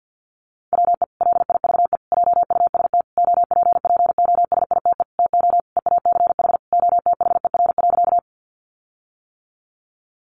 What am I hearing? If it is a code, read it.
Morse code, heard as a keyed tone, about 50 wpm, 700 Hz. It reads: WE LIVE JUST OPPOSITE TO EACH OTHER.
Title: Sentences from Top 1300 Words 50wpm